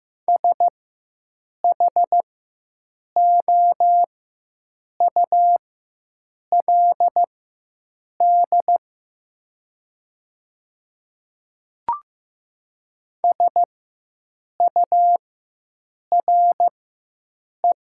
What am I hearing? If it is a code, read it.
Morse code, heard as a keyed tone, about 15 wpm, spaced as for 7 wpm, 700 Hz. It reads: SHOULD SURE
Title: 4x Character Spacing - Top 300 Words - 15wpm